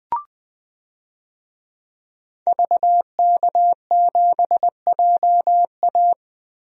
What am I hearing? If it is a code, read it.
Morse code, heard as a keyed tone, about 21 wpm, 700 Hz. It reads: VK7JA
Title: DXCC-100 Call Signs - 5 Characters - 20wpm